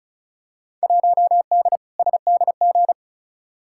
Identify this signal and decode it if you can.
Morse code, heard as a keyed tone, about 36 wpm, 700 Hz. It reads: 1D SDG
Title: Field Day Exchanges - 35wpm